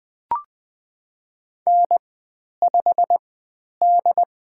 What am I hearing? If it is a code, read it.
Morse code, heard as a keyed tone, about 20 wpm, spaced as for 10 wpm, 700 Hz. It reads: N5D
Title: 095 Lesson 16g - Letter c - Call Signs 20wpm